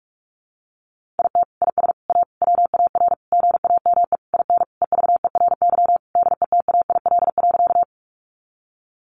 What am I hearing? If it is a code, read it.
Morse code, heard as a keyed tone, about 44 wpm, 700 Hz. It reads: IT IS A WAR GAME IN EVERY DETAIL.
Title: Sentences from Top 1500 Words 45wpm